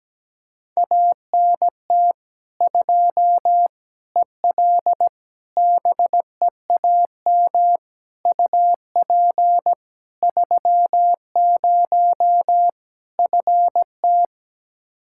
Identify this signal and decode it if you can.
Morse code, heard as a keyed tone, about 17 wpm, 700 Hz. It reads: ANT 2 EL BEAM UP 30 FT
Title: K1USN SST QSO Practice 17wpm